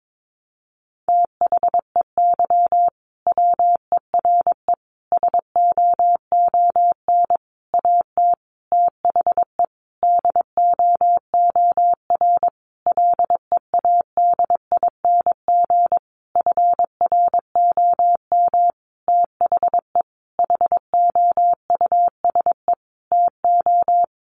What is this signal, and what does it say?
Morse code, heard as a keyed tone, about 22 wpm, 700 Hz. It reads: THEY WERE SOON AT THE DOOR LEADING FROM THE HOUSE TO
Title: Sentences from Top 1200 Words 22wpm